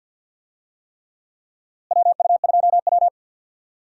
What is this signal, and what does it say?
Morse code, heard as a keyed tone, about 47 wpm, 700 Hz. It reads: WU2W
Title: USA Call Signs - 4 Characters - Repeated 3x - 50wpm